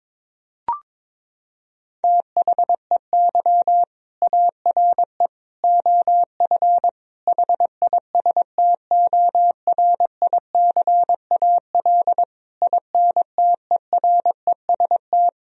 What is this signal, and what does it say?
Morse code, heard as a keyed tone, about 22 wpm, 700 Hz. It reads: THEY ARE OF HISTORICAL INTEREST
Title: Sentences from Top 1000 Words - Review Part 4 - 22wpm